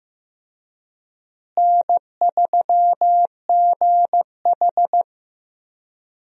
Morse code, heard as a keyed tone, about 15 wpm, 700 Hz. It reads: N3GH
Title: USA Call Signs - 4 Characters - Repeated 3x - 15wpm